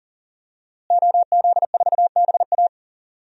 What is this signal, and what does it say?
Morse code, heard as a keyed tone, about 40 wpm, 700 Hz. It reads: OZ4BA
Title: DXCC-100 Call Signs - 5 Characters - Repeated 3x - 40wpm